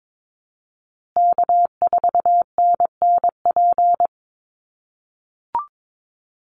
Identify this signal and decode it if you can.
Morse code, heard as a keyed tone, about 22 wpm, 700 Hz. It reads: K4NNP
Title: USA Call Signs Repeated 3x 22wpm